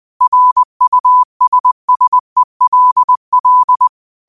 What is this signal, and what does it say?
Morse code, heard as a keyed tone, about 20 wpm, 1000 Hz. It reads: RUSSELL
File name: qotdfiles / 20wpm